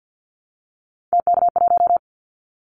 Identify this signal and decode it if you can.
Morse code, heard as a keyed tone, about 50 wpm, 700 Hz. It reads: TK1
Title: Callsign Prefixes for Top 100 DXCC - 50wpm